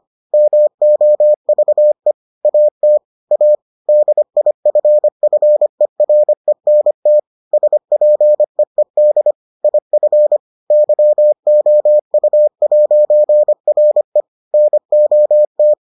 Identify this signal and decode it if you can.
Morse code, heard as a keyed tone, about 25 wpm, 600 Hz. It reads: MOVE AT A DIFFERENT SPEED IF YOU'RE NOT